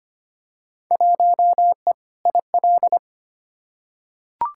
Morse code, heard as a keyed tone, about 25 wpm, 700 Hz. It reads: 1E IL
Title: Field Day Exchanges - 25wpm